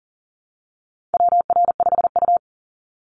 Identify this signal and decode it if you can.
Morse code, heard as a keyed tone, about 42 wpm, 700 Hz. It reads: WR5U
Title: Call Signs - 40wpm